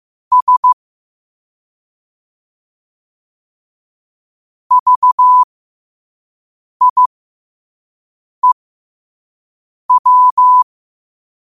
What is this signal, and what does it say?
Morse code, heard as a keyed tone, about 15 wpm, spaced as for 5 wpm, 1000 Hz. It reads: S VIEW
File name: qotdfiles / 5wpm